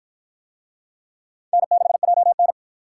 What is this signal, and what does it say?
Morse code, heard as a keyed tone, about 46 wpm, 700 Hz. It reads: N6JN